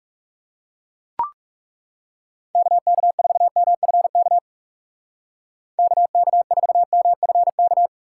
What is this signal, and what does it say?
Morse code, heard as a keyed tone, about 36 wpm, 700 Hz. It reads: KK4MFK KK4MFK
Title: Speed Racing - USA 2x3 Call Signs 30wpm